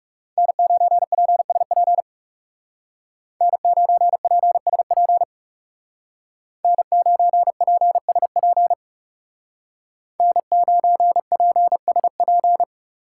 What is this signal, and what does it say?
Morse code, heard as a keyed tone, about 37 wpm, 700 Hz. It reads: N9PSP N9PSP N9PSP N9PSP
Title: Speed Racing - USA 1x3 Call Signs 30wpm